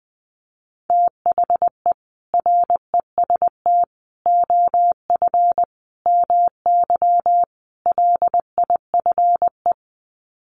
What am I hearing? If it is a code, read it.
Morse code, heard as a keyed tone, about 20 wpm, 700 Hz. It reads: THE REST OF MY LIFE
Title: Sets of 5 Words 20wpm